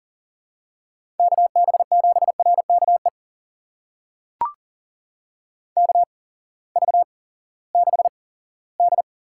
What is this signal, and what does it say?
Morse code, heard as a keyed tone, about 40 wpm, spaced as for 20 wpm, 700 Hz. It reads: KB7RKE KV6D
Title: USA Call Signs - 8x Character Spacing - 40wpm